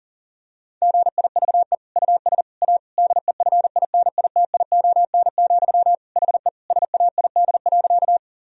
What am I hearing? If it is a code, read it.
Morse code, heard as a keyed tone, about 40 wpm, 700 Hz. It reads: GIVE US A DEFINITION, HE SAID.
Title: Sentences from Top 1000 Words - Review Part 4 - 40wpm